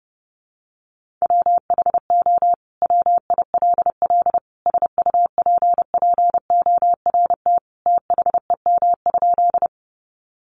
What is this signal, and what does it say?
Morse code, heard as a keyed tone, about 30 wpm, 700 Hz. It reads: WHO WILL SUPPORT THEM?
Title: Sentences from Top 800 Words 30wpm